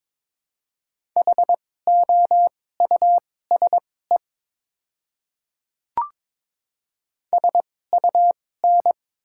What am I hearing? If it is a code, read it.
Morse code, heard as a keyed tone, about 21 wpm, spaced as for 16 wpm, 700 Hz. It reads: HOUSE SUN
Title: 2x Character Spacing - Top 300 Words - 22wpm